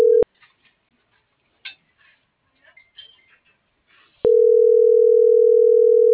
Ringback tone (North America), 440 + 480 Hz.